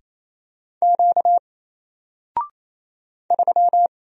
Morse code, heard as a keyed tone, about 27 wpm, 700 Hz. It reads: Q 3
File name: Single-Letter-Number-28wpm